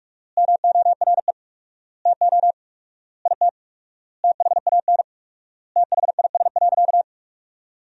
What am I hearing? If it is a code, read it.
Morse code, heard as a keyed tone, about 43 wpm, 700 Hz. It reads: MORE TO IT THAN THIS.
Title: 4x Word-Spacing - Sentences from Top 100 Words - 45wpm